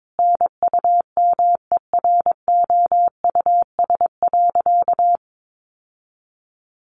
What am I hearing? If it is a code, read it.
Morse code, heard as a keyed tone, about 22 wpm, 700 Hz. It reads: NUMEROUS.
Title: Sentences from Top 1400 Words 22wpm